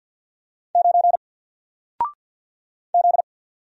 Morse code, heard as a keyed tone, about 50 wpm, 700 Hz. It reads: 9 Z
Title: Single Letter-Number Alt 50wpm